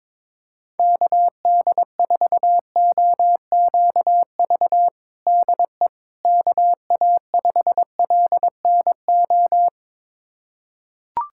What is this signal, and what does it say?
Morse code, heard as a keyed tone, about 22 wpm, 700 Hz. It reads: KD4OQV DE KA5LNO
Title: USA Callsign DE Callsign - 6 Characters - 22wpm